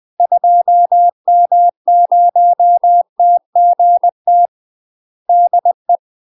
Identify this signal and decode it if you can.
Morse code, heard as a keyed tone, about 20 wpm, 700 Hz. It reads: 2M0TGT DE
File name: UK MASTER QSOs TRAINING / 20WPM / UK MASTER QSOs 0020